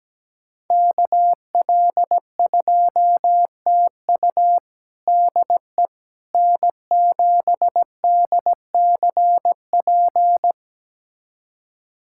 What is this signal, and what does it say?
Morse code, heard as a keyed tone, about 17 wpm, 700 Hz. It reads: KL2TU DE N7DCP
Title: USA Callsign DE Callsign - 5 Characters - 17wpm